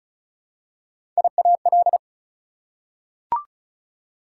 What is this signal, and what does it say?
Morse code, heard as a keyed tone, about 34 wpm, 700 Hz. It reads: IAL
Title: Prefixes and Suffixes 35wpm